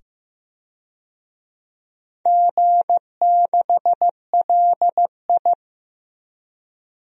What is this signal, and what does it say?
Morse code, heard as a keyed tone, about 15 wpm, 700 Hz. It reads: G6LI
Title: DXCC-100 Call Signs - 4 Characters - 15wpm